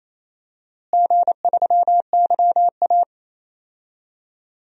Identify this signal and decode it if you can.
Morse code, heard as a keyed tone, about 27 wpm, 700 Hz. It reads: G3YA
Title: DXCC-100 Call Signs - 4 Characters - Repeated 3x - 28wpm